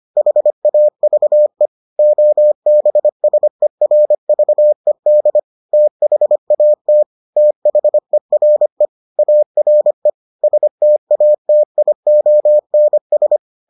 Morse code, heard as a keyed tone, about 25 wpm, 600 Hz. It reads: HAVE OBSERVED THAT THERE ARE STATIONS